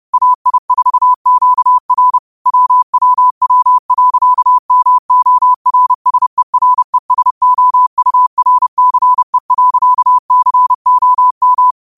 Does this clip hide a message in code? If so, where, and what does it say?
Yes: Morse code from 0.1 to 11.7 s, heard as a keyed tone, about 30 wpm, 1000 Hz. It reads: AI4QR WWW.MORSERESOURCE.COM